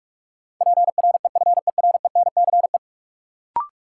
Morse code, heard as a keyed tone, about 46 wpm, 700 Hz. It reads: PREFERENCE
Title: Sentences from Top 1500 Words - New Word Drill - 45wpm